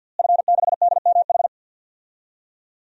Morse code, heard as a keyed tone, about 53 wpm, 700 Hz. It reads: F6DMH
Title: DXCC-100 Call Signs - 5 Characters - Repeated 3x - 50wpm